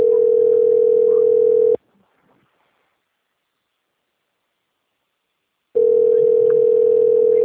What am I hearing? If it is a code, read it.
Ringback tone (North America), 440 + 480 Hz.
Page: Goiania - GO